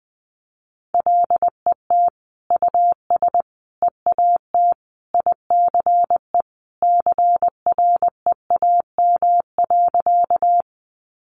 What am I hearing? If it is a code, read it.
Morse code, heard as a keyed tone, about 20 wpm, 700 Hz. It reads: LET US EAT ICE CREAM.
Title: Sentences from Top 1500 Words 20wpm